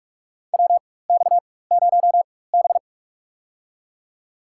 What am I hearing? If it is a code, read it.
Morse code, heard as a keyed tone, about 42 wpm, spaced as for 21 wpm, 700 Hz. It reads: WX0B
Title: USA Call Signs - 4 Characters - 4x Character Spacing - 45wpm